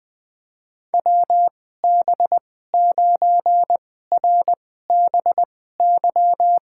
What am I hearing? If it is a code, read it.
Morse code, heard as a keyed tone, about 20 wpm, spaced as for 14 wpm, 700 Hz. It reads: WB9RBY
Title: USA Call Signs - 2x Character Spacing - 20wpm